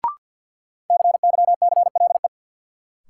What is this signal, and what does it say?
Morse code, heard as a keyed tone, about 53 wpm, 700 Hz. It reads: CYCLE
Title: Sentences from Top 900 Words - New Word Drill - 50wpm